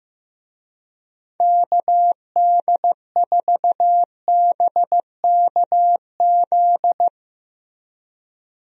Morse code, heard as a keyed tone, about 15 wpm, 700 Hz. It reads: KD4BKZ
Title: USA Call Signs - 6 Characters - Repeated 3x - 15wpm